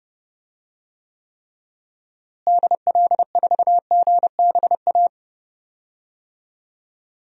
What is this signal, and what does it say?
Morse code, heard as a keyed tone, about 30 wpm, 700 Hz. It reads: DL4GBA
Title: DXCC-100 Call Signs - 6 Characters - Repeated 3x - 30wpm